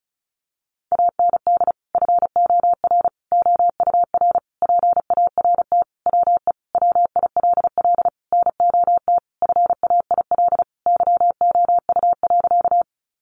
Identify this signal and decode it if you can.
Morse code, heard as a keyed tone, about 34 wpm, 700 Hz. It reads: AND FOR OUR PART WE WILL NOT FAIL YOU.
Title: Sentences from Top 1100 Words 35wpm